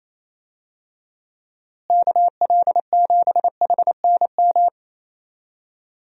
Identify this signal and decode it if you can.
Morse code, heard as a keyed tone, about 28 wpm, 700 Hz. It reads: KL7HNM
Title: USA Call Signs - 6 Characters - Repeated 3x - 28wpm